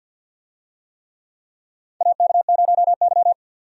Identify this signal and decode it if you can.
Morse code, heard as a keyed tone, about 53 wpm, 700 Hz. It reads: AK0Y
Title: USA Call Signs - 4 Characters - 50wpm